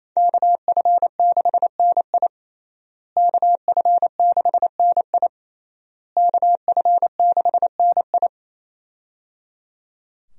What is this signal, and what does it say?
Morse code, heard as a keyed tone, about 27 wpm, 700 Hz. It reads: KF6NI KF6NI KF6NI
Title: USA Call Signs - 5 Characters - Repeated 3x - 28wpm